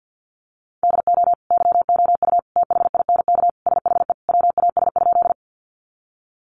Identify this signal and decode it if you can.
Morse code, heard as a keyed tone, about 50 wpm, 700 Hz. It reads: DO YOU THINK SHE WAS?
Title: Sentences from Top 200 Words 50wpm